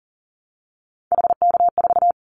Morse code, heard as a keyed tone, about 40 wpm, 700 Hz. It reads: HK4
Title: Callsign Prefixes for Top 100 DXCC - 40wpm